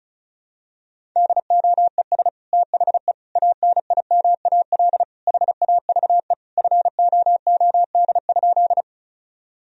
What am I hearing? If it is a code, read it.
Morse code, heard as a keyed tone, about 34 wpm, 700 Hz. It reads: DOES THE ANIMAL HAVE FOOD?
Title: Sentences from Top 1000 Words - Review Part 1 - 35wpm